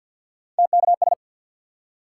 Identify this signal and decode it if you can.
Morse code, heard as a keyed tone, about 53 wpm, 700 Hz. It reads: TKS
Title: Top 180 QSO Elements 50wpm